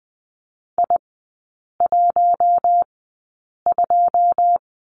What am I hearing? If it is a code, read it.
Morse code, heard as a keyed tone, about 20 wpm, spaced as for 8 wpm, 700 Hz. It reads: I12
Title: 081 Lesson 14e - Number 2 - Three Characters 20wpm